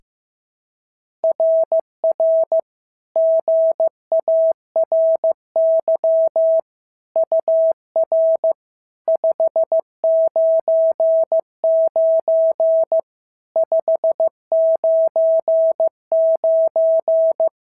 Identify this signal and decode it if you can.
Morse code, heard as a keyed tone, about 15 wpm, 650 Hz. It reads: RR GARY UR 599 599
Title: SOTA QSO Practice 15wpm